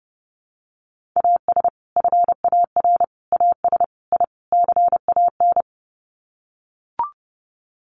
Morse code, heard as a keyed tone, about 30 wpm, 700 Hz. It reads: AS FAR AS I CAN
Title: Sets of 5 Words Encore 30wpm